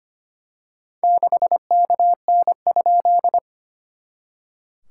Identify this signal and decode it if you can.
Morse code, heard as a keyed tone, about 25 wpm, 700 Hz. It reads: 6KN?
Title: SOTA QSO Practice 25wpm